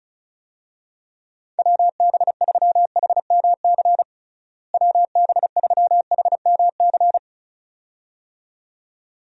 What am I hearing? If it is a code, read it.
Morse code, heard as a keyed tone, about 36 wpm, 700 Hz. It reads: WB3HMC WB3HMC
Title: USA Call Signs - 6 Characters - Repeated 2x - 35wpm